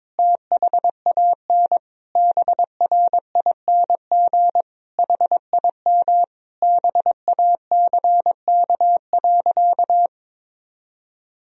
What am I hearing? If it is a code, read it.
Morse code, heard as a keyed tone, about 22 wpm, 700 Hz. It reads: THAN BRING HIM BACK.
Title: Sentences from Top 1000 Words - Review Part 3 - 22wpm